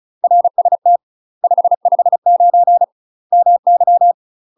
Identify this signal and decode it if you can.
Morse code, heard as a keyed tone, about 34 wpm, 700 Hz. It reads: RST 559 MY
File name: UK MASTER QSOs TRAINING / 35WPM / UK MASTER QSOs 0003